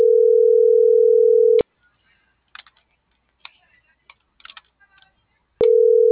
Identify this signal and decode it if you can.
Ringback tone (North America), 440 + 480 Hz.